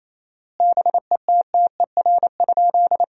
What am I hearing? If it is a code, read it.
Morse code, heard as a keyed tone, about 27 wpm, 700 Hz. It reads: BETTER?
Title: 4x Word-Spacing - Sentences from Top 400 Words - 28wpm